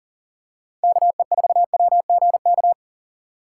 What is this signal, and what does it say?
Morse code, heard as a keyed tone, about 40 wpm, 700 Hz. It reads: KE4WGK
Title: USA Call Signs - 6 Characters - 3x Character Spacing - 40wpm